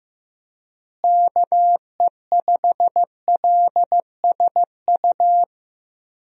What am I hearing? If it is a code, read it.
Morse code, heard as a keyed tone, about 15 wpm, 700 Hz. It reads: KE5LSU
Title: USA Call Signs - 6 Characters - Repeated 2x - 15wpm